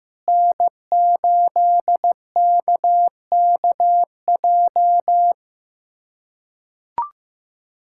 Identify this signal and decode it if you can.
Morse code, heard as a keyed tone, about 16 wpm, 700 Hz. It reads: N8KKJ
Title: USA Callsign DE Callsign - 5 Characters - 15wpm